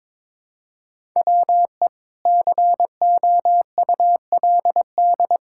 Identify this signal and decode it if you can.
Morse code, heard as a keyed tone, about 22 wpm, 700 Hz. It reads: WE COULD
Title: Pangram Challenge v2 - 22wpm